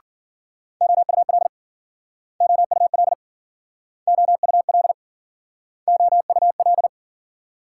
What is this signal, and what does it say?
Morse code, heard as a keyed tone, about 52 wpm, 700 Hz. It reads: OUL OUL OUL OUL